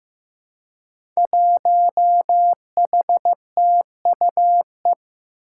Morse code, heard as a keyed tone, about 15 wpm, 700 Hz. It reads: 1HTUE
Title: Five Challenging Letter-Number Combinations 15wpm